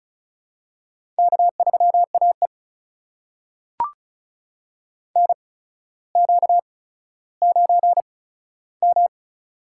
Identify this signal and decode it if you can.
Morse code, heard as a keyed tone, about 36 wpm, spaced as for 15 wpm, 700 Hz. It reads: K3AE NQ9M
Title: USA Call Signs - 8x Character Spacing - 35wpm